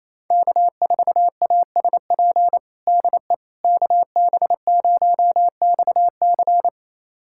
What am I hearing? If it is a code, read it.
Morse code, heard as a keyed tone, about 28 wpm, 700 Hz. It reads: K4ASP DE KB0XC
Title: USA Callsign DE Callsign - 5 Characters - 28wpm